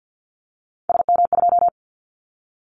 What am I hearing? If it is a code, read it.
Morse code, heard as a keyed tone, about 49 wpm, 700 Hz. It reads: SM2
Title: Callsign Prefixes for Top 25 DXCC - Repeated 3x - 50wpm